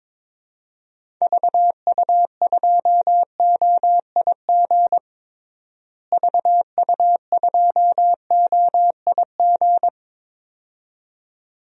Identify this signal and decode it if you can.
Morse code, heard as a keyed tone, about 22 wpm, 700 Hz. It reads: VU2OIG VU2OIG
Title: DXCC-100 Call Signs - 6 Characters - Repeated 2x - 22wpm